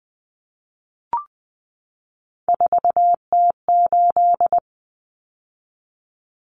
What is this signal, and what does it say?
Morse code, heard as a keyed tone, about 20 wpm, 700 Hz. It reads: 4T8
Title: Common Cut Numbers - 3-Digit 20wpm